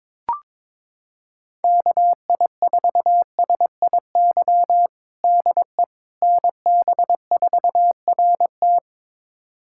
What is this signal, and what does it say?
Morse code, heard as a keyed tone, about 21 wpm, 700 Hz. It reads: KI4SIY DE NB4RT
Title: USA Callsign DE Callsign 22wpm